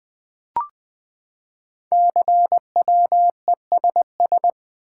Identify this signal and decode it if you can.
Morse code, heard as a keyed tone, about 20 wpm, 700 Hz. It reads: CWESS
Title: Code Groups - 5 Letters - 20wpm